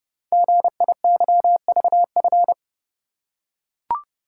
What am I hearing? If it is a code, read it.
Morse code, heard as a keyed tone, about 28 wpm, 700 Hz. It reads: GIYVF
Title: Code Groups - 5 Letters - 30wpm